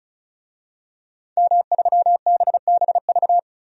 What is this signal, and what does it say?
Morse code, heard as a keyed tone, about 34 wpm, 700 Hz. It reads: M3BBV
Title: DXCC-100 Call Signs - 5 Characters - 35wpm